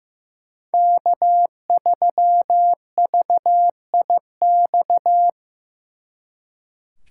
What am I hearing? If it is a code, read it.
Morse code, heard as a keyed tone, about 18 wpm, 700 Hz. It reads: K3VIX
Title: Speed Racing - USA 1x3 Call Signs 15wpm